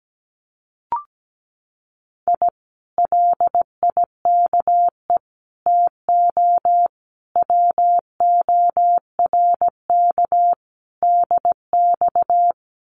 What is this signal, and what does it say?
Morse code, heard as a keyed tone, about 17 wpm, 700 Hz. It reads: I LIKE TO WORK DX
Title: Common QSO Phrases Encore 17wpm